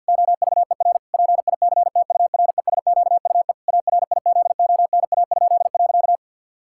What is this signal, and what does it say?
Morse code, heard as a keyed tone, about 52 wpm, 700 Hz. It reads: OVER PICTURESQUE ARIZONA?.
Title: Pangram Challenge v2 - 50wpm